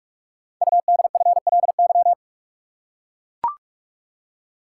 Morse code, heard as a keyed tone, about 44 wpm, 700 Hz. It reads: UDFLY
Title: Code Groups - 5 Letters - 45wpm